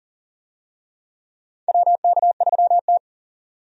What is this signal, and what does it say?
Morse code, heard as a keyed tone, about 40 wpm, 700 Hz. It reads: WK3T
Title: USA Call Signs - 4 Characters - 40wpm